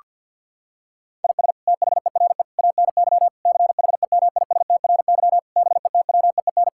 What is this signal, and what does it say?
Morse code, heard as a keyed tone, about 47 wpm, 700 Hz. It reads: IS THERE ANY CHEMISTRY BETWEEN
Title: Sentences from Top 1000 Words - Review Part 6 - 50wpm